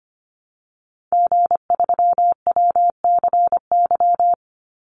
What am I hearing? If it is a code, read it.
Morse code, heard as a keyed tone, about 25 wpm, 700 Hz. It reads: G3WCY
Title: Call Signs - 25wpm